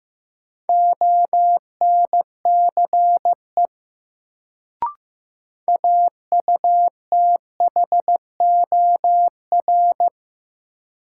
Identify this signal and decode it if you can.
Morse code, heard as a keyed tone, about 15 wpm, 700 Hz. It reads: ONCE AUTHOR
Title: Sets of 4 Words - New Word Drill - 15wpm